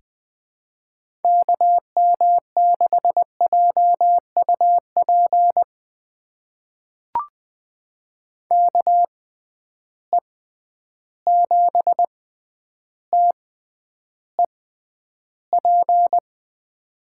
Morse code, heard as a keyed tone, about 20 wpm, spaced as for 10 wpm, 700 Hz. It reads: KM6JUP KE7TEP
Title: USA Call Signs - 6 Characters - 6x Character Spacing - 20wpm